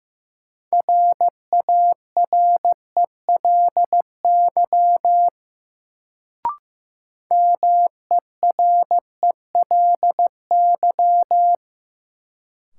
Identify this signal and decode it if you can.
Morse code, heard as a keyed tone, about 15 wpm, 700 Hz. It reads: RARELY MERELY
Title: Sentences from Top 1400 Words - New Word Drill Alt - 15wpm